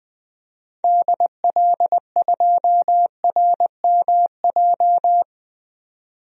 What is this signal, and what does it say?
Morse code, heard as a keyed tone, about 21 wpm, 700 Hz. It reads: DL2RMJ
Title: DXCC-100 Call Signs - 6 Characters - 20wpm